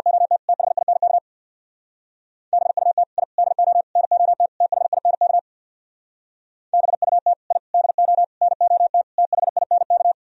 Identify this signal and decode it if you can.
Morse code, heard as a keyed tone, about 55 wpm, 700 Hz. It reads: OT THINK BUT I DO NOT THINK BUT I DO NOT THINK
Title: Speed-Racing - Sets of 5 Words 50wpm